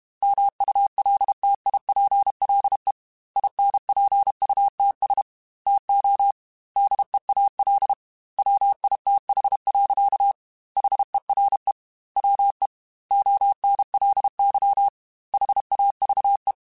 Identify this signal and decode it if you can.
Morse code, heard as a keyed tone, about 32 wpm, 800 Hz. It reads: MULTIPLE INPUTS TO DEAL WITH. HERE WE ONLY HAVE
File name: Technical Subjects / Technical Subjects-mp3s / AND Gate @30